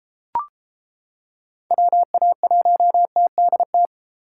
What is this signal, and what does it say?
Morse code, heard as a keyed tone, about 28 wpm, 700 Hz. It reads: WA1TDT